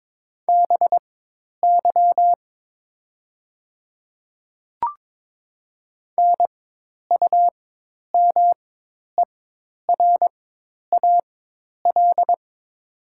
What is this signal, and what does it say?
Morse code, heard as a keyed tone, about 22 wpm, spaced as for 10 wpm, 700 Hz. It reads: BY NUMERAL
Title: 4x Character Spacing - Top 500 Words - 22wpm